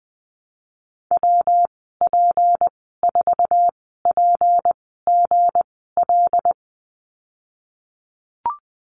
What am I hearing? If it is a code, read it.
Morse code, heard as a keyed tone, about 20 wpm, spaced as for 14 wpm, 700 Hz. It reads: WP4PGL
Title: 173 Lesson 28g - Number 9 - Call Signs 20wpm